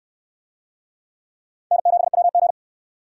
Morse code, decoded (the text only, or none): N7WG